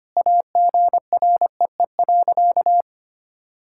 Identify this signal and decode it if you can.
Morse code, heard as a keyed tone, about 25 wpm, 700 Hz. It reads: AGREE.
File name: Sentences with 3-Word Phrases - 25wpm